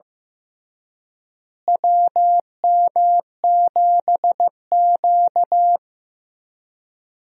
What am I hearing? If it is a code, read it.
Morse code, heard as a keyed tone, about 15 wpm, 700 Hz. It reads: WM7Q
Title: USA Call Signs - 4 Characters - Repeated 3x - 15wpm